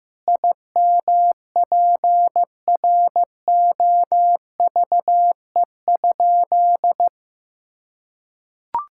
Morse code, heard as a keyed tone, about 15 wpm, 700 Hz. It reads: IMPROVE?
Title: Sentences from Top 1000 Words - Review Part 5 - 15wpm